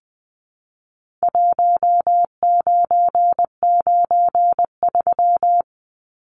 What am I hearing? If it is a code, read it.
Morse code, heard as a keyed tone, about 20 wpm, 700 Hz. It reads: 1993
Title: Numbers - Contest Style - 4-Digit 20wpm